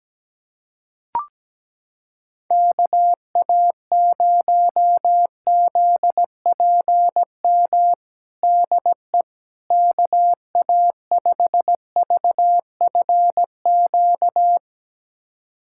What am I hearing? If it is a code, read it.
Morse code, heard as a keyed tone, about 17 wpm, 700 Hz. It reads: KA0ZPM DE KA5VFQ
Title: USA Callsign DE Callsign - 6 Characters - 17wpm